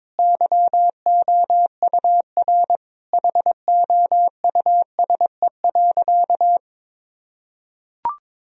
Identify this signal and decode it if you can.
Morse code, heard as a keyed tone, about 22 wpm, 700 Hz. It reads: YOUR HOUSE.
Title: Sentences from Top 1000 Words - Review Part 2 - 22wpm